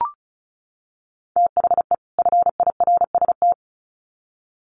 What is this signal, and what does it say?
Morse code, heard as a keyed tone, about 34 wpm, 700 Hz. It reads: THE FIRST
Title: Sets of 2 Words 35wpm